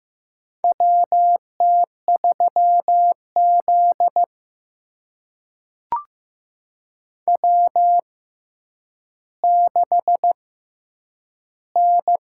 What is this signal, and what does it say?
Morse code, heard as a keyed tone, about 15 wpm, spaced as for 9 wpm, 700 Hz. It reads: WT3Z W6N
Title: USA Call Signs - 4 Characters - 6x Character Spacing - 15wpm